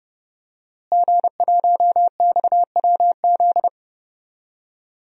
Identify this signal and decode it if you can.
Morse code, heard as a keyed tone, about 30 wpm, 700 Hz. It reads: G1XWZ
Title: DXCC-100 Call Signs - 5 Characters - Repeated 3x - 30wpm